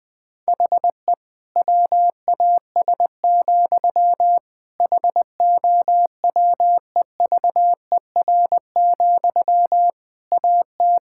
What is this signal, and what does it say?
Morse code, heard as a keyed tone, about 20 wpm, 700 Hz. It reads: HE WAS, HOWEVER, AT
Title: Sentences from Top 1000 Words - Review Part 5 - 20wpm